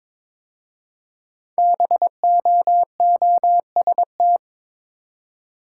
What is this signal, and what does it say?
Morse code, heard as a keyed tone, about 22 wpm, 700 Hz. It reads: BOOST
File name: 5 Letter Words - First 3 Letters Given - 22wpm